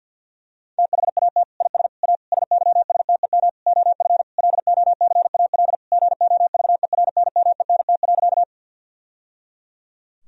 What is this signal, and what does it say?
Morse code, heard as a keyed tone, about 50 wpm, 700 Hz. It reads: THAT IS A SYSTEM OF LOCAL GOVERNMENT.
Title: Sentences from Top 1000 Words - Review Part 3 - 50wpm